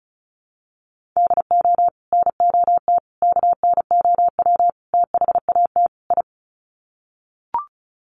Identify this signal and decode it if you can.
Morse code, heard as a keyed tone, about 34 wpm, 700 Hz. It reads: DO NOT KNOW THAT I
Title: Sets of 5 Words Encore2 35wpm